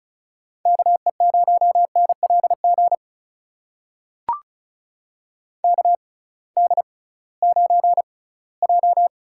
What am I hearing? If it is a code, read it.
Morse code, heard as a keyed tone, about 34 wpm, spaced as for 20 wpm, 700 Hz. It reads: KE0NLG KD9J
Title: USA Call Signs - 6 Characters - 6x Character Spacing - 35wpm